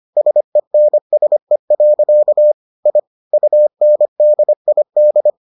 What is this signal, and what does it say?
Morse code, heard as a keyed tone, about 25 wpm, 600 Hz. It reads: SENSE. I UNDID